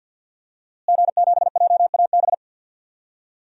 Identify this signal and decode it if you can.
Morse code, heard as a keyed tone, about 51 wpm, 700 Hz. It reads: G7JAB
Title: DXCC-100 Call Signs - 5 Characters - Repeated 2x - 50wpm